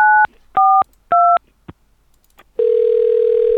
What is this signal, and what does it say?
DTMF, heard as telephone touch tones, tones of about 280 ms. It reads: 942